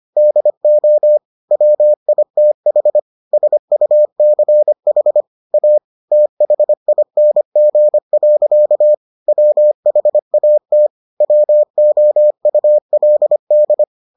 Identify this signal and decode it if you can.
Morse code, heard as a keyed tone, about 25 wpm, 600 Hz. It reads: DO WITH SUCH A THING. WHAT WOULD